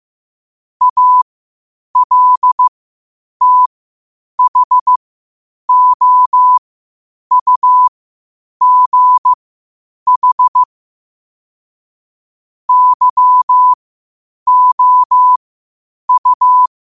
Morse code, heard as a keyed tone, about 15 wpm, spaced as for 8 wpm, 1000 Hz. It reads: ALTHOUGH YOU
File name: qotdfiles / 8wpm